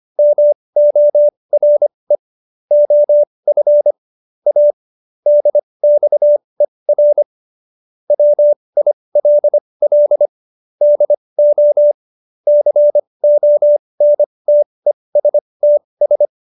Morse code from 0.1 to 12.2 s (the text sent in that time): MORE OF A DXER WILL DO